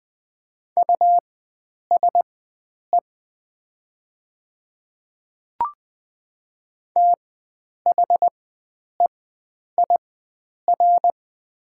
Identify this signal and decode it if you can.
Morse code, heard as a keyed tone, about 20 wpm, spaced as for 9 wpm, 700 Hz. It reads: USE THEIR
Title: 4x Character Spacing - Top 100 Words - 20wpm